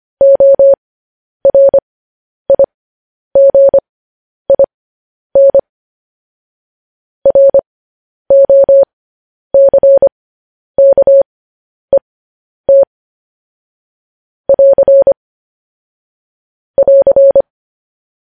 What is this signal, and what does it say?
Morse code, heard as a keyed tone, about 25 wpm, spaced as for 10 wpm, 550 Hz. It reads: ORIGIN ROCKET <AR> <AR>
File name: podcast / News headlines-1x-25x10